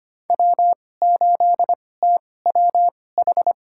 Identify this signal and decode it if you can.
Morse code, heard as a keyed tone, about 25 wpm, spaced as for 18 wpm, 700 Hz. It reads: W8TWH
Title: USA Call Signs - 5 Characters - 2x Character Spacing - 25wpm